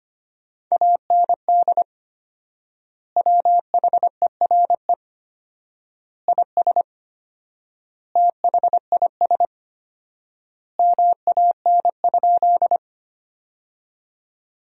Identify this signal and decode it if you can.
Morse code, heard as a keyed tone, about 25 wpm, 700 Hz. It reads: AND WHERE IS THIS MAN?
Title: 4x Word-Spacing - Sentences from Top 200 Words - 25wpm